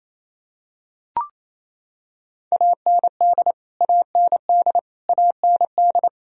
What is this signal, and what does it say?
Morse code, heard as a keyed tone, about 28 wpm, 700 Hz. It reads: AND AND AND
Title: Top 100 Words Repeated 28wpm